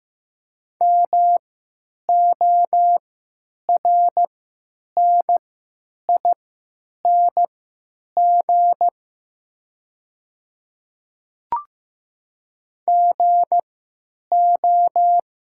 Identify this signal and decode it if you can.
Morse code, heard as a keyed tone, about 15 wpm, spaced as for 9 wpm, 700 Hz. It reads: MORNING GO
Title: 3x Character Spacing - Top 500 Words - 15wpm